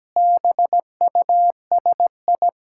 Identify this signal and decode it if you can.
Morse code, heard as a keyed tone, about 17 wpm, 700 Hz. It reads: BUSI
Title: Sentences from Top 1000 Words - Review Part 6 - 17wpm